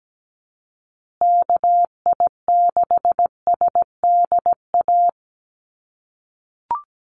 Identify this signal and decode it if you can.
Morse code, heard as a keyed tone, about 17 wpm, 700 Hz. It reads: KI6SDA
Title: USA Call Signs Repeated 3x 17wpm